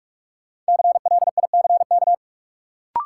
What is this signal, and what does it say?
Morse code, heard as a keyed tone, about 44 wpm, 700 Hz. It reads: CLICK
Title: Sentences from Top 1100 Words - New Word Drill Alt - 45wpm